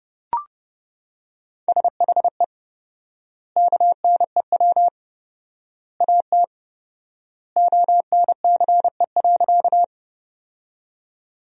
Morse code, heard as a keyed tone, about 32 wpm, 700 Hz. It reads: SHE KNEW AT ONCE.
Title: 4x Word-Spacing - Sentences from Top 400 Words - 30wpm